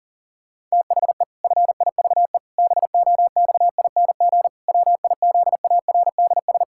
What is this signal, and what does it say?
Morse code, heard as a keyed tone, about 40 wpm, 700 Hz. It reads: THE FIVE BOXING WIZARDS
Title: Pangram Challenge v2 - 40wpm